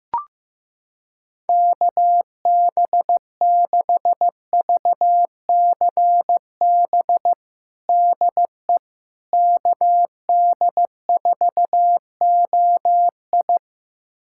Morse code, heard as a keyed tone, about 16 wpm, 700 Hz. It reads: KB6VCB DE KD4OI
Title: USA Callsign DE Callsign 15wpm